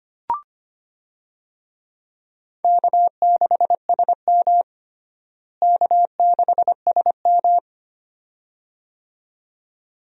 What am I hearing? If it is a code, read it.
Morse code, heard as a keyed tone, about 25 wpm, 700 Hz. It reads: K6SM K6SM
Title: USA Call Signs - 4 Characters - Repeated 2x - 25wpm